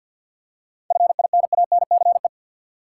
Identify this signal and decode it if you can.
Morse code, heard as a keyed tone, about 48 wpm, 700 Hz. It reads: FINANCE
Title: Seven Letter Words - 50wpm